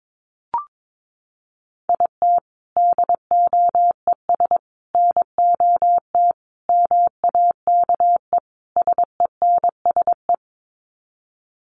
Morse code, heard as a keyed tone, about 22 wpm, 700 Hz. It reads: IT DOES NOT MAKE SENSE
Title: Sets of 5 Words Encore2 22wpm